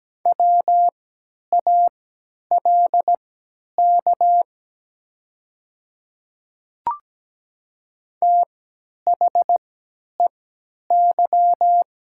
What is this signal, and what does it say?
Morse code, heard as a keyed tone, about 17 wpm, spaced as for 10 wpm, 700 Hz. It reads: WALK THEY
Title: 3x Character Spacing - Top 300 Words - 17wpm